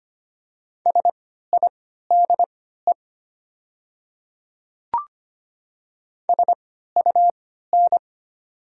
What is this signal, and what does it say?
Morse code, heard as a keyed tone, about 25 wpm, spaced as for 14 wpm, 690 Hz. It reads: SIDE SUN
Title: 3x Character Spacing - Top 300 Words - 25wpm